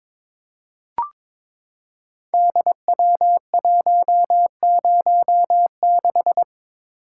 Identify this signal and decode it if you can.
Morse code, heard as a keyed tone, about 22 wpm, 700 Hz. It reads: DW106
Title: Code Groups - 5 Letter:Numbers - 22wpm